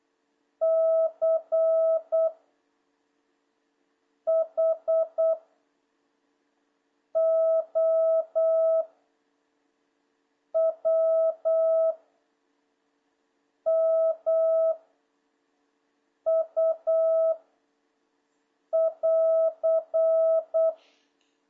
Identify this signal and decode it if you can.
Morse code, heard as a keyed tone, about 8 wpm, 660 Hz. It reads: C H O W M U <AR>
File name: jh8cbh / 2026cw